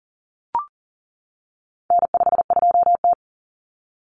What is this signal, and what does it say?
Morse code, heard as a keyed tone, about 40 wpm, 700 Hz. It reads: N52T
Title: Common Cut Numbers - 4-Digit 40wpm